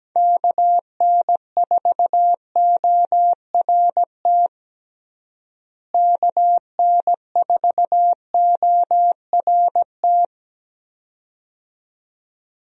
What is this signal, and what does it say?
Morse code, heard as a keyed tone, about 17 wpm, 700 Hz. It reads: KN4ORT KN4ORT
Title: USA Call Signs - 6 Characters - Repeated 2x - 17wpm